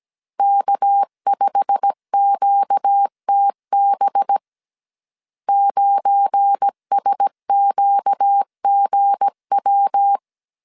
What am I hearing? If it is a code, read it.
Morse code, heard as a keyed tone, about 17 wpm, 800 Hz. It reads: K5QTB 9SQGW
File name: CW / DET / 1 9 1500C S C 16wpm 2909